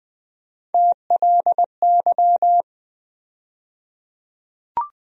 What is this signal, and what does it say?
Morse code, heard as a keyed tone, about 20 wpm, 700 Hz. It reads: TLY
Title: USA Call Sign Postfixes - 3 Letters 20wpm